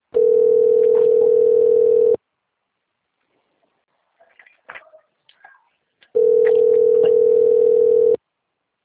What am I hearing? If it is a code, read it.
Ringback tone (North America), 440 + 480 Hz.
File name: rg-1600-unknown- - - .9069